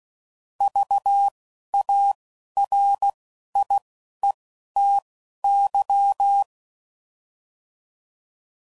Morse code, heard as a keyed tone, about 16 wpm, spaced as for 11 wpm, 800 Hz. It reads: VARIETY
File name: Prefixes / By First Letter / Vs